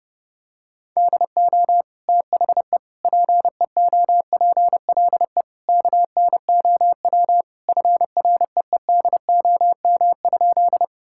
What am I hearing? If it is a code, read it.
Morse code, heard as a keyed tone, about 32 wpm, 700 Hz. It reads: DO THE PEOPLE KNOW FREEDOM?
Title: Sentences from Top 1000 Words - Review Part 5 - 30wpm